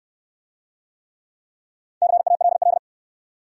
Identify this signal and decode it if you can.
Morse code, heard as a keyed tone, about 67 wpm, 700 Hz. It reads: CALL